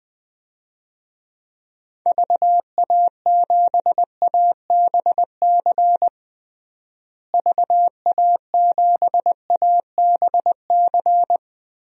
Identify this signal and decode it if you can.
Morse code, heard as a keyed tone, about 20 wpm, 700 Hz. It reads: VA7ABC VA7ABC
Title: DXCC-100 Call Signs - 6 Characters - Repeated 2x - 20wpm